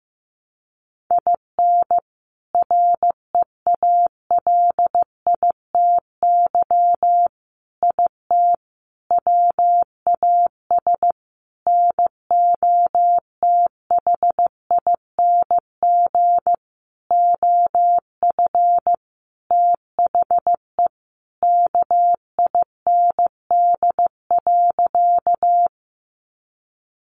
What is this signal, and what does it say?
Morse code, heard as a keyed tone, about 15 wpm, 700 Hz. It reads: IN REALITY IT WAS NOTHING OF THE KIND.
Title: Sentences from Top 1200 Words 15wpm